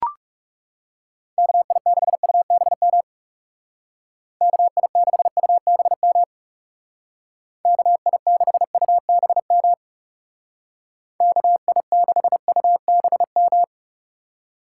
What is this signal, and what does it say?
Morse code, heard as a keyed tone, about 36 wpm, 700 Hz. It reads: KI6UBM KI6UBM KI6UBM KI6UBM
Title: Speed Racing - USA 2x3 Call Signs 30wpm